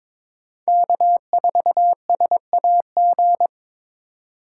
Morse code, heard as a keyed tone, about 22 wpm, 700 Hz. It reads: K4SAG
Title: USA Call Signs - 5 Characters - 6x Character Spacing - 22wpm